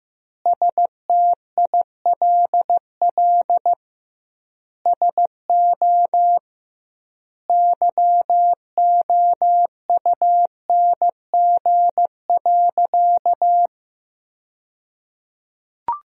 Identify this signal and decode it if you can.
Morse code, heard as a keyed tone, about 16 wpm, 700 Hz. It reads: STILL SO YOUNG.
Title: 2x Word-Spacing - Sentences from Top 400 Words - 15wpm